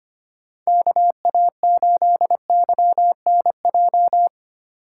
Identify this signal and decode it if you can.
Morse code, heard as a keyed tone, about 25 wpm, 700 Hz. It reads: KA8YNJ
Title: USA Call Signs - 4x Character Spacing - 25wpm